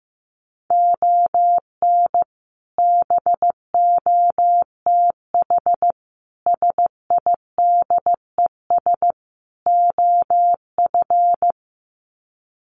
Morse code, heard as a keyed tone, about 15 wpm, 700 Hz. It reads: ON BOTH SIDES OF
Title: Sets of 4 Words 15wpm